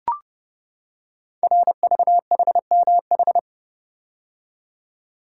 Morse code, heard as a keyed tone, about 31 wpm, 700 Hz. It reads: RVHMH
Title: Code Groups - 5 Letters - 30wpm